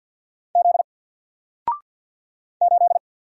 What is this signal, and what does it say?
Morse code, heard as a keyed tone, about 50 wpm, 700 Hz. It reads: Z 8
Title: Single Letter-Number 50wpm